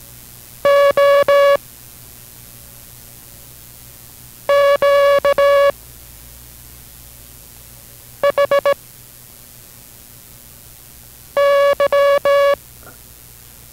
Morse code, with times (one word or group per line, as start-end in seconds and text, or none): 0.6-1.6 O
4.5-5.7 Q
8.2-8.7 H
11.4-12.5 Y